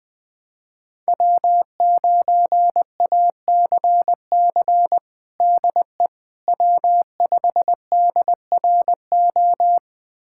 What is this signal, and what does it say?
Morse code, heard as a keyed tone, about 20 wpm, 700 Hz. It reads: W9ACC DE W5DRO
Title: USA Callsign DE Callsign - 5 Characters - 20wpm